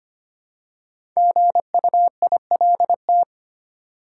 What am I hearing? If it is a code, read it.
Morse code, heard as a keyed tone, about 25 wpm, 700 Hz. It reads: GUILT
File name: 5 Letter Words - First 3 Letters Given - 25wpm